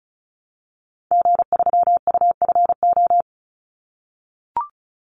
Morse code, heard as a keyed tone, about 36 wpm, 700 Hz. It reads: G3UFO
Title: Call Signs - 35wpm